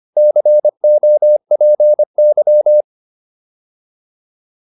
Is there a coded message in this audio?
Yes: Morse code from 0.2 to 2.8 s, heard as a keyed tone, about 25 wpm, 600 Hz. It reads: COPY